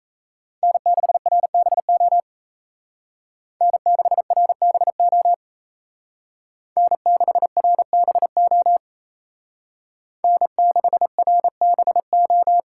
Morse code, heard as a keyed tone, about 33 wpm, 700 Hz. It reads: N6RBO N6RBO N6RBO N6RBO
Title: Speed Racing - USA Call Signs 28wpm